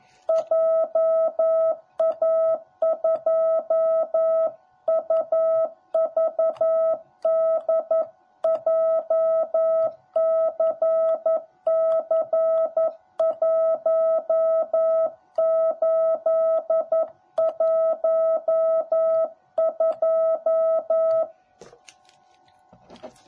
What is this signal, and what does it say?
Morse code, heard as a keyed tone, about 11 wpm, 650 Hz. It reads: JA2UVD JCC1812
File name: jh8cbh / 2021cw / cw10b